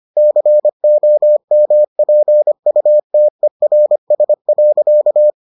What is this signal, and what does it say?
Morse code, heard as a keyed tone, about 25 wpm, 600 Hz. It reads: COMPUTERS.